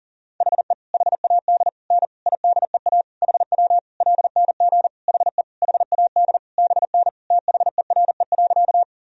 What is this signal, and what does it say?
Morse code, heard as a keyed tone, about 40 wpm, 700 Hz. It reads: HE HAD N IDEA HW LNG HE HAD BN THERE.
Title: Sentences from Top 300 Words and Abbreviations 40wpm